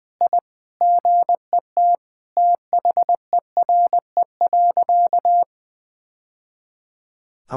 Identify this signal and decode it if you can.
Morse code, heard as a keyed tone, about 20 wpm, 700 Hz. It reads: I GET THERE.
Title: Sentences with 2-Word Phrases - 20wpm